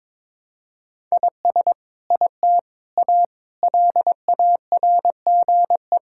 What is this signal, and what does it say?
Morse code, heard as a keyed tone, about 22 wpm, 700 Hz. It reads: IS IT A LARGE
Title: Sentences from Top 1000 Words - Review Part 5 - 22wpm